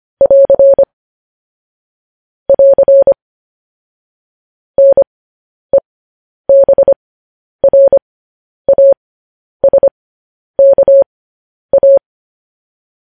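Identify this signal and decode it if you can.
Morse code, heard as a keyed tone, about 25 wpm, spaced as for 10 wpm, 550 Hz. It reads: <AR> <AR> NEBRASKA
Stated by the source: Morse code news headlines sent at 10.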